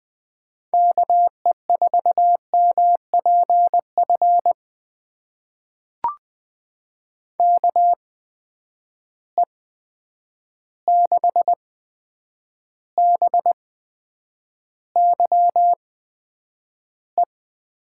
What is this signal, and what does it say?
Morse code, heard as a keyed tone, about 21 wpm, spaced as for 9 wpm, 700 Hz. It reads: KE4MPF KE6BYE
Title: USA Call Signs - 6 Characters - 8x Character Spacing - 20wpm